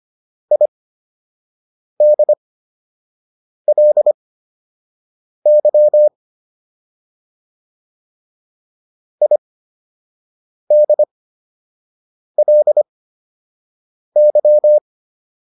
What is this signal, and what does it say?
Morse code, heard as a keyed tone, about 25 wpm, spaced as for 6 wpm, 600 Hz. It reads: IDLY IDLY